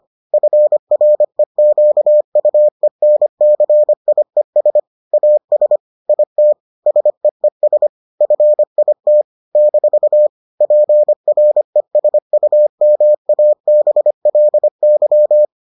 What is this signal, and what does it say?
Morse code, heard as a keyed tone, about 25 wpm, 600 Hz. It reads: FREQUENCIES AS IT SEES FIT - PRESUMABLY